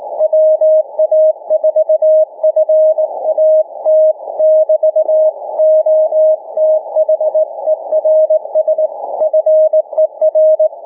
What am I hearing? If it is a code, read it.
Morse code, heard as a keyed tone, about 19 wpm, 590 Hz. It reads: WA4FAT <BT> OTHERS FER
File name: cw / ssn 40M